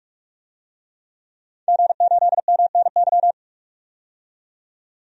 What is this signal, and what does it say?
Morse code, heard as a keyed tone, about 44 wpm, 700 Hz. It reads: G8MNY
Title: DXCC-100 Call Signs - 5 Characters - 45wpm